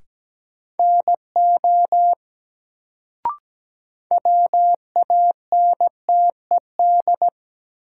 Morse code, heard as a keyed tone, about 17 wpm, 700 Hz. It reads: NO WANTED
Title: Sets of 2 Words - New Word Drill - 17wpm